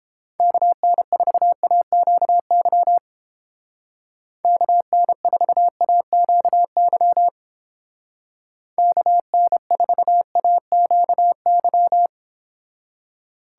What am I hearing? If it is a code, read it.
Morse code, heard as a keyed tone, about 29 wpm, 700 Hz. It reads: KN4AQY KN4AQY KN4AQY